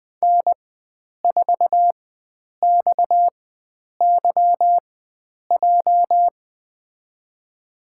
Morse code, heard as a keyed tone, about 20 wpm, spaced as for 9 wpm, 700 Hz. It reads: N4XYJ
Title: USA Call Signs - 4x Character Spacing - 20wpm